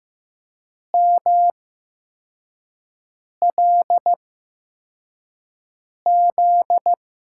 Morse code, heard as a keyed tone, about 15 wpm, spaced as for 4 wpm, 700 Hz. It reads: MLZ
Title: USA Call Signs - 6 Characters - 8x Character Spacing - 15wpm